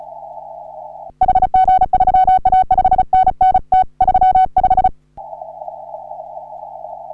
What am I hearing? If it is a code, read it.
Morse code, heard as a keyed tone, about 34 wpm, 750 Hz. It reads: HG3A 5NN T35
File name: CW / HA3OU RHOBCW REC